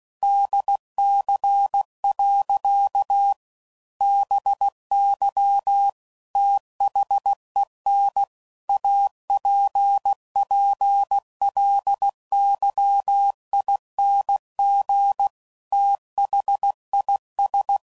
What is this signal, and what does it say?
Morse code, heard as a keyed tone, about 16 wpm, 800 Hz. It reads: DC. BY THEN APPLYING THIS
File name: CW Code Groups @15WPM / Technical Subjects / PwrSupply